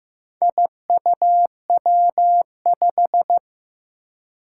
Morse code, heard as a keyed tone, about 15 wpm, 700 Hz. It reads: IUW5
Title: Four Challenging Letter-Number Combinations 15wpm